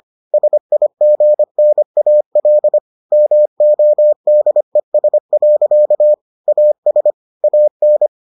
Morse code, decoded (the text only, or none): SIGNAL MODES. AS AN